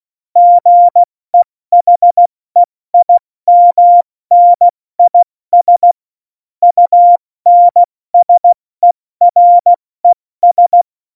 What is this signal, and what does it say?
Morse code, heard as a keyed tone, about 16 wpm, 700 Hz. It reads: GEHEIMNIS UNSERES